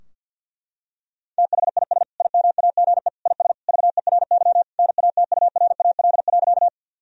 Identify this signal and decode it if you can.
Morse code, heard as a keyed tone, about 51 wpm, 700 Hz. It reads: THIS IMAGE IS VERY NATURAL.
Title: Sentences from Top 1000 Words - Review Part 4 - 50wpm